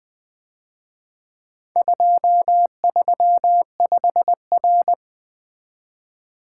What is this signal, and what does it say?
Morse code, heard as a keyed tone, about 24 wpm, 700 Hz. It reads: 235R
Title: Speed-Racing - Four Letter-Number Combinations - 20wpm